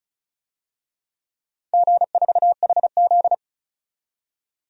Morse code, heard as a keyed tone, about 36 wpm, 700 Hz. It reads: G4HZ
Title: DXCC-100 Call Signs - 4 Characters - 35wpm